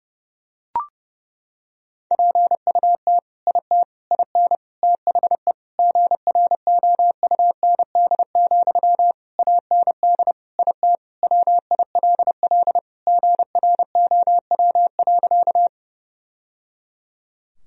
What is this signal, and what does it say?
Morse code, heard as a keyed tone, about 32 wpm, 700 Hz. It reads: PUT IT IN THE GROUND, AND IT WILL GROW.
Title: Sentences from Top 1000 Words - Review Part 2 - 30wpm